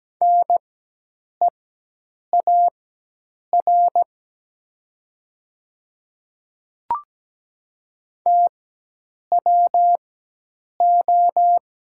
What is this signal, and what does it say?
Morse code, heard as a keyed tone, about 17 wpm, spaced as for 8 wpm, 700 Hz. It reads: NEAR TWO
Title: 4x Character Spacing - Top 200 Words - 17wpm